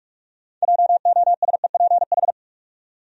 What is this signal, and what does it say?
Morse code, heard as a keyed tone, about 46 wpm, 700 Hz. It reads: JOSEPH
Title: USA Common Names - 45wpm